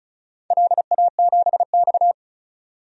Morse code, heard as a keyed tone, about 34 wpm, 700 Hz. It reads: LA7X
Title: DXCC-100 Call Signs - 4 Characters - 35wpm